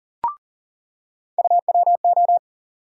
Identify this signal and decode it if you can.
Morse code, heard as a keyed tone, about 40 wpm, 700 Hz. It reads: UWO
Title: USA Call Sign Postfixes - 3 Letters 40wpm